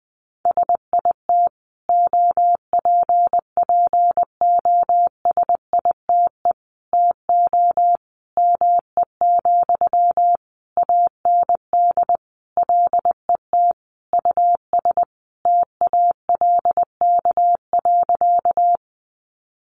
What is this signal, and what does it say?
Morse code, heard as a keyed tone, about 20 wpm, 700 Hz. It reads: SIT OPPOSITE TO ME, AND LET US TALK.
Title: Sentences from Top 1300 Words 20wpm